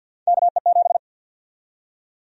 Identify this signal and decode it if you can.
Morse code, heard as a keyed tone, about 51 wpm, 700 Hz. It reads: KE7
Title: USA Call Sign Prefixes - 50wpm